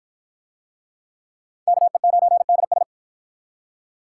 Morse code, heard as a keyed tone, about 51 wpm, 700 Hz. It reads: KE9DS